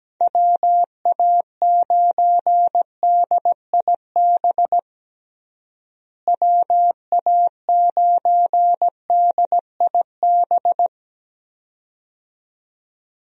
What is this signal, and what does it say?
Morse code, heard as a keyed tone, about 17 wpm, 700 Hz. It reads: WA9DIB WA9DIB
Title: USA Call Signs - 6 Characters - Repeated 2x - 17wpm